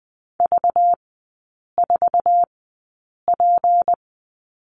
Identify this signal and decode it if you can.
Morse code, heard as a keyed tone, about 20 wpm, spaced as for 8 wpm, 700 Hz. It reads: V4P
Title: 191 Lesson 31e - Letter v - Three Characters 20wpm